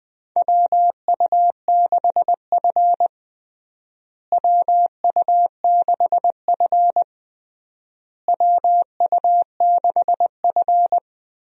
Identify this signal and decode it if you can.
Morse code, heard as a keyed tone, about 21 wpm, 700 Hz. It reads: WU6F WU6F WU6F
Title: USA Call Signs - 4 Characters - Repeated 3x - 20wpm